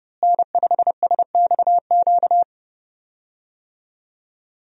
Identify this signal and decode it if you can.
Morse code, heard as a keyed tone, about 31 wpm, 700 Hz. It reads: N5SXQ
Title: USA Call Signs - 5 Characters - Repeated 2x - 30wpm